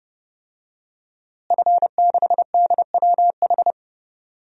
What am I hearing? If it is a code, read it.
Morse code, heard as a keyed tone, about 30 wpm, 700 Hz. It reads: F6DWH
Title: DXCC-100 Call Signs - 5 Characters - Repeated 2x - 30wpm